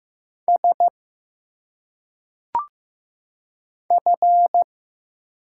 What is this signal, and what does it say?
Morse code, heard as a keyed tone, about 15 wpm, 700 Hz. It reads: S F
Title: Single Letters - Phonetic Pronunciation 15wpm